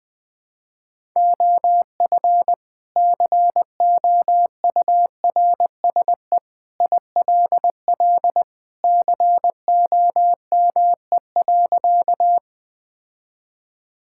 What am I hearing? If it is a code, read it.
Morse code, heard as a keyed tone, about 21 wpm, 700 Hz. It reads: OF COURSE ILL COME.
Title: Sentences with 2-Word Phrases - 20wpm